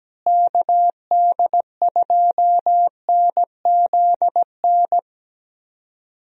Morse code, heard as a keyed tone, about 17 wpm, 700 Hz. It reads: KD2NZN
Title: USA Call Signs - 6 Characters - Repeated 2x - 17wpm